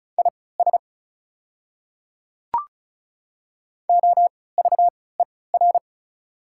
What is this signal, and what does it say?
Morse code, heard as a keyed tone, about 36 wpm, spaced as for 20 wpm, 690 Hz. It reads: IS OVER
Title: 3x Character Spacing - Top 100 Words - 35wpm